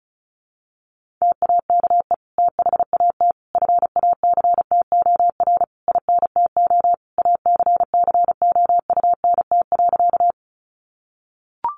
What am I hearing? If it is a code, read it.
Morse code, heard as a keyed tone, about 34 wpm, 700 Hz. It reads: TAKE THAT FACTOR INTO ACCOUNT.
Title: Sentences from Top 1300 Words 35wpm